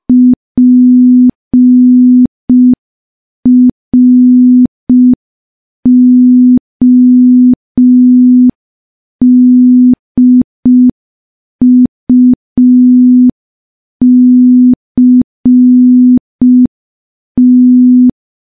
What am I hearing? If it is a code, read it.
Morse code, heard as a keyed tone, about 5 wpm, 250 Hz. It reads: PRODUCT
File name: morse / morse-training-05WPM